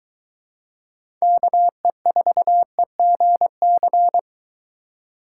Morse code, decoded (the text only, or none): KE4EGC